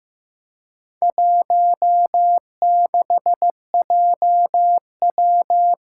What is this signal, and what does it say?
Morse code, heard as a keyed tone, about 15 wpm, 700 Hz. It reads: 16JW
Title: Four Challenging Letter-Number Combinations 15wpm